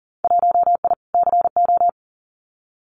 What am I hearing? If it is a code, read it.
Morse code, heard as a keyed tone, about 41 wpm, 700 Hz. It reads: 1I CO
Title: Winter Field Day Exchanges Encore 40wpm